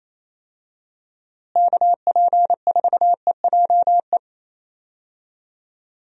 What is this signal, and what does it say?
Morse code, heard as a keyed tone, about 28 wpm, 700 Hz. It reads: KP4EJE
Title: DXCC-100 Call Signs - 6 Characters - Repeated 3x - 28wpm